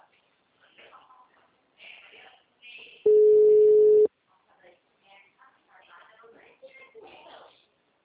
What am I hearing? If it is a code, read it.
Ringback tone (Europe, 425 Hz), with speech.